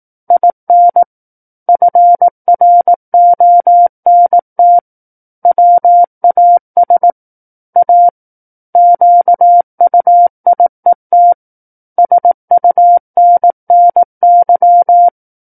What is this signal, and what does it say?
Morse code, heard as a keyed tone, about 18 wpm, 700 Hz. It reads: IN FRONT WAS A QUIET SUNNY
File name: 18 WPM